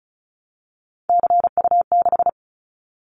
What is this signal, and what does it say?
Morse code, heard as a keyed tone, about 36 wpm, 700 Hz. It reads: CU6
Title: Callsign Prefixes for Top 50 DXCC - 35wpm